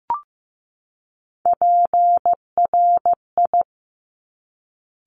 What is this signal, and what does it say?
Morse code, heard as a keyed tone, about 15 wpm, 700 Hz. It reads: PRI
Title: Top Three Letter Combinations in Words 15wpm